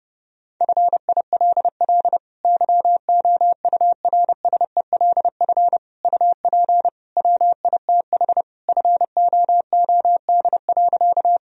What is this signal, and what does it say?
Morse code, heard as a keyed tone, about 30 wpm, 700 Hz. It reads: FILL YOURSELF UP WITH FOOD.
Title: Sentences from Top 1000 Words - Review Part 3 - 30wpm